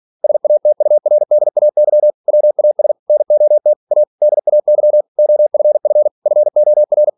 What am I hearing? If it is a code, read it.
Morse code, heard as a keyed tone, about 46 wpm, 600 Hz. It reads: SATURDAY WAS NOT A DAY OFF FOR
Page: Never 47wpm